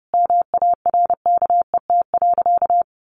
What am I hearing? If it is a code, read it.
Morse code, heard as a keyed tone, about 30 wpm, 700 Hz. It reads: MARKET.
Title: Sentences from Top 1400 Words 30wpm